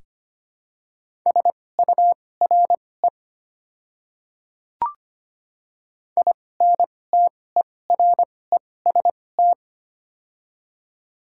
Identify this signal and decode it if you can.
Morse code, heard as a keyed tone, about 25 wpm, spaced as for 18 wpm, 700 Hz. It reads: SURE INTEREST
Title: 2x Character Spacing - Top 500 Words - 25wpm